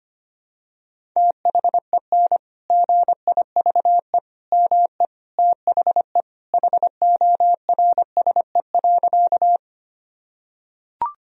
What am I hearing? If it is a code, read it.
Morse code, heard as a keyed tone, about 25 wpm, 700 Hz. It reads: THEN GIVE ME THE HORSE.
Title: Sentences from Top 1000 Words - Review Part 1 - 25wpm